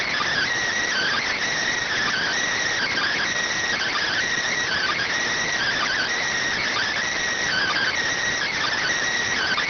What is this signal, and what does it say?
Emergency Alert System (SAME) tones (header bursts), heard as harsh buzzing bursts. It is heard from 1.0 to 8.6 s.